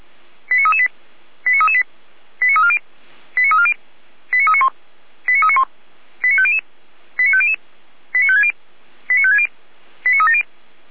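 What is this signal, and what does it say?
Five-tone selective call (ZVEI), heard as quick runs of five tones, tones of about 70 ms. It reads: 89338 89338 89340 89340 89391 89391 89500 89500 89560 89560 89370
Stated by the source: ZVEI five-tone selective call received via SDR.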